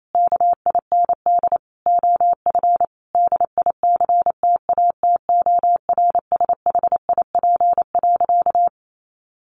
Morse code, heard as a keyed tone, about 28 wpm, 700 Hz. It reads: KIND OF DICTATORSHIP.
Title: A Sound Of Thunder 28wpm